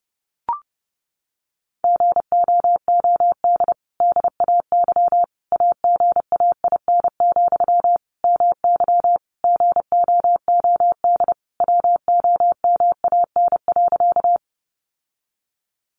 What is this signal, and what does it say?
Morse code, heard as a keyed tone, about 28 wpm, 700 Hz. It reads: GOOD DAY AGAIN, MY GOOD WOMAN.
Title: Sentences from Top 1100 Words 30wpm